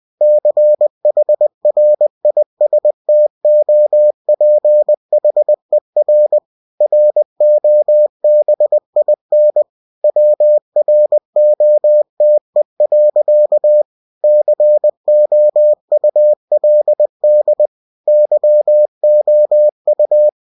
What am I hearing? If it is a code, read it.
Morse code, heard as a keyed tone, about 20 wpm, 600 Hz. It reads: CHRISTOPHER ROBIN WROTE. COULD YOU
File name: cw-books / winnie-the-pooh / winnie-20wpm-page-0052